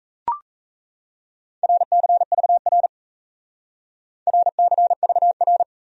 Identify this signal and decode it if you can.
Morse code, heard as a keyed tone, about 35 wpm, 700 Hz. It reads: RCVR RCVR
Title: Speed-Racing Abbreviations - 28wpm